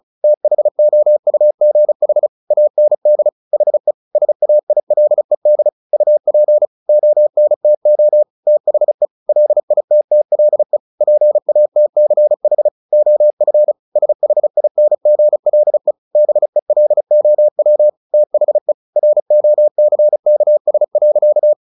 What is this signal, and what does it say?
Morse code, heard as a keyed tone, about 34 wpm, 600 Hz. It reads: THOUGH AND HE SAILED UP ONTO THE LITTLE PATCH OF SHINGLE BELOW THE ROCKS.
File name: the old man and the sea 2 35wpm